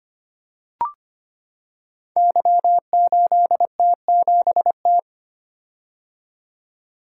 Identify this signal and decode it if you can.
Morse code, heard as a keyed tone, about 25 wpm, 700 Hz. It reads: Y8T7T
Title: Code Groups - 5 Letter:Numbers - 25wpm